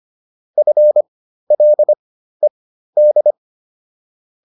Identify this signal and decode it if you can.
Morse code, heard as a keyed tone, about 25 wpm, spaced as for 13 wpm, 600 Hz. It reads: FLED